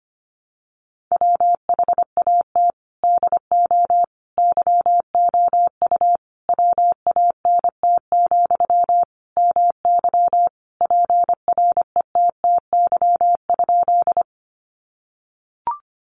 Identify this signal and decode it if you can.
Morse code, heard as a keyed tone, about 25 wpm, 700 Hz. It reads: WHAT DO YOU WANT, MY PRETTY?
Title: Sentences from Top 1000 Words 25wpm